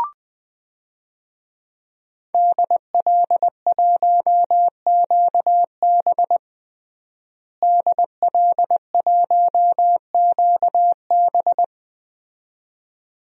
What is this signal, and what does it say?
Morse code, heard as a keyed tone, about 20 wpm, 700 Hz. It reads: DL1QB DL1QB
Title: DXCC-100 Call Signs - 5 Characters - Repeated 2x - 20wpm